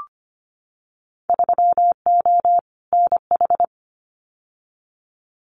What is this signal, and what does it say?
Morse code, heard as a keyed tone, about 25 wpm, 700 Hz. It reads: 3O NH
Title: Winter Field Day Exchanges Encore 25wpm